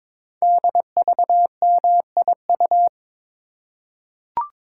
Morse code, heard as a keyed tone, about 22 wpm, 700 Hz. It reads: DVMIU
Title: Code Groups - 5 Letters - 22wpm